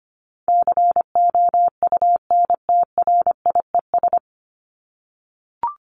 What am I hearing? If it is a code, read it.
Morse code, heard as a keyed tone, about 25 wpm, 700 Hz. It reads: COUNTRIES
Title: ES:S Suffix - Top 100 Words with Base Words 25wpm